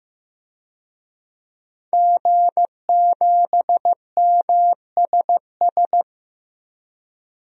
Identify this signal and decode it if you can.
Morse code, heard as a keyed tone, about 15 wpm, 700 Hz. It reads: G7MSS
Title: DXCC-100 Call Signs - 5 Characters - Repeated 2x - 15wpm